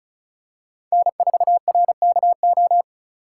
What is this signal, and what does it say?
Morse code, heard as a keyed tone, about 34 wpm, 700 Hz. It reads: N4RKO
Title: USA Call Signs - 5 Characters - 4x Character Spacing - 35wpm